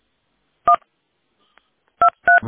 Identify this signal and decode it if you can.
DTMF, heard as telephone touch tones, tones of about 100 ms. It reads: 123